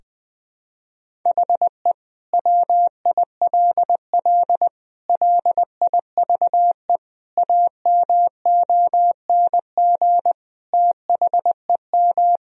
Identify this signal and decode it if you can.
Morse code, heard as a keyed tone, about 20 wpm, 700 Hz. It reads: HE WILL LIVE AMONG THEM
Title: Sentences from Top 1000 Words - Review Part 3 - 20wpm